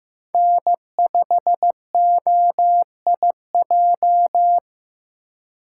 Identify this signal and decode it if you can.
Morse code, heard as a keyed tone, about 18 wpm, 700 Hz. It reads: N5OIJ
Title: Speed Racing - USA 1x3 Call Signs 15wpm